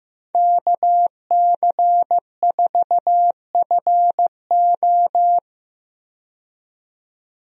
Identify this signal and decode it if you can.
Morse code, heard as a keyed tone, about 15 wpm, 700 Hz. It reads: KC4FO
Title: USA Call Signs - 5 Characters - 15wpm